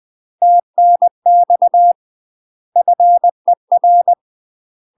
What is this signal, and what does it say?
Morse code, heard as a keyed tone, about 20 wpm, 700 Hz. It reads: TNX FER
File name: UK MASTER QSOs TRAINING / 20WPM / UK MASTER QSOs 0049